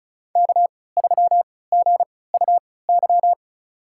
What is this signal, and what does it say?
Morse code, heard as a keyed tone, about 34 wpm, spaced as for 20 wpm, 700 Hz. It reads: K3GUY
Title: USA Call Signs - 5 Characters - 3x Character Spacing - 35wpm